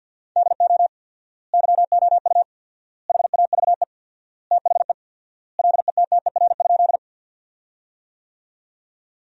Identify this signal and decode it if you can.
Morse code, heard as a keyed tone, about 45 wpm, 700 Hz. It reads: DO YOU HAVE THE LETTER?
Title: 4x Word-Spacing - Sentences from Top 300 Words - 50wpm